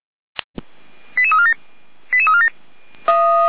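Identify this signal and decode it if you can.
Five-tone selective call (ZVEI), heard as quick runs of five tones, tones of about 70 ms. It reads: 88347 88347
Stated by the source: ZVEI five-tone call 88109 received on SDR.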